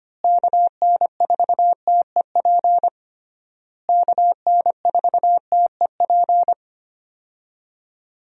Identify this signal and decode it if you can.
Morse code, heard as a keyed tone, about 25 wpm, 700 Hz. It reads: KN4TEP KN4TEP
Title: USA Call Signs - 6 Characters - Repeated 3x - 25wpm